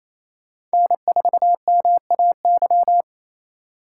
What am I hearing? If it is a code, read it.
Morse code, heard as a keyed tone, about 36 wpm, 700 Hz. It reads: N4MAY